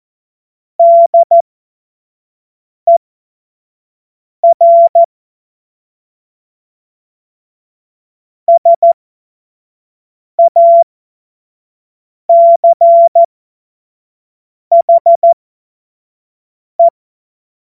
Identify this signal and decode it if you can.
Morse code, heard as a keyed tone, about 14 wpm, spaced as for 5 wpm, 680 Hz. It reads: DER SACHE
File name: Morsekurs / Witze / Witz 21. Februar 2023 25 F70